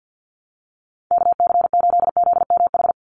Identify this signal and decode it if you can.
Morse code, heard as a keyed tone, about 49 wpm, 700 Hz. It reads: KC8ZMH
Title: USA Call Signs 50wpm